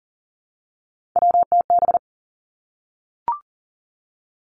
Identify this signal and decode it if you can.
Morse code, heard as a keyed tone, about 40 wpm, 700 Hz. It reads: WTB
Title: Three Challenging Letter-Number Combinations 40wpm